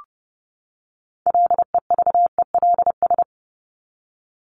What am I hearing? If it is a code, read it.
Morse code, heard as a keyed tone, about 30 wpm, 700 Hz. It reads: LEVELS
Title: ES:S Suffix - Top 100 Words with Base Words 30wpm